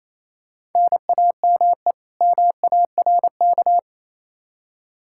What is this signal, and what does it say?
Morse code, heard as a keyed tone, about 28 wpm, 700 Hz. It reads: NAME MARK
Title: Practice QSO - Exchange 1 - Beginning - 28wpm